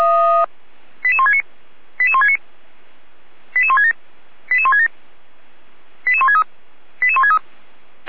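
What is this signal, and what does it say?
Five-tone selective call (ZVEI), heard as quick runs of five tones, tones of about 70 ms. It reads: 88169 88169 88167 88167 88163 88163
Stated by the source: ZVEI five-tone selective call received on an SDR.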